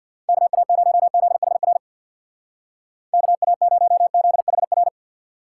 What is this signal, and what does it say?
Morse code, heard as a keyed tone, about 61 wpm, 700 Hz. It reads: KA0ZHR KA0ZHR
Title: Speed Racing - USA Call Signs 50wpm